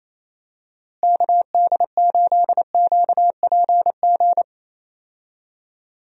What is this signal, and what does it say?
Morse code, heard as a keyed tone, about 28 wpm, 700 Hz. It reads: KD8QPG
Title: USA Call Signs - 6 Characters - Repeated 3x - 28wpm